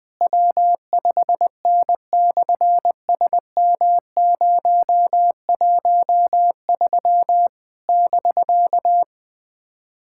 Morse code, heard as a keyed tone, about 20 wpm, 700 Hz. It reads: W5N/SM013 <BK>
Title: SOTA QSO Practice 20wpm